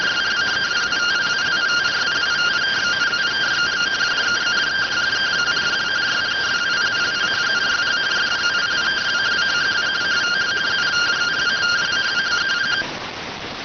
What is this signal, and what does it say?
Two-tone teleprinter signal, about 170 Hz shift.